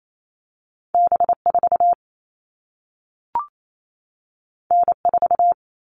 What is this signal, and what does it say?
Morse code, heard as a keyed tone, about 28 wpm, 700 Hz. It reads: B4 N4
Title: All Cut Numbers - 2-Digit 28wpm